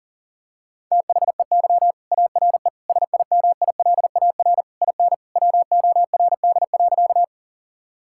Morse code, heard as a keyed tone, about 37 wpm, 700 Hz. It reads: THEY ARE SIMILAR IN WORD.
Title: Sentences from Top 1000 Words - Review Part 5 - 40wpm